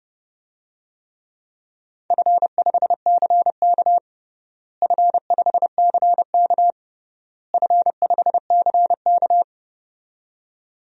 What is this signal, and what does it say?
Morse code, heard as a keyed tone, about 32 wpm, 700 Hz. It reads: F5CK F5CK F5CK
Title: DXCC-100 Call Signs - 4 Characters - Repeated 3x - 30wpm